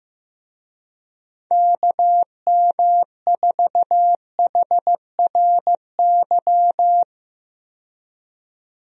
Morse code, heard as a keyed tone, about 15 wpm, 700 Hz. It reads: KM4HRY
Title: USA Call Signs - 6 Characters - 15wpm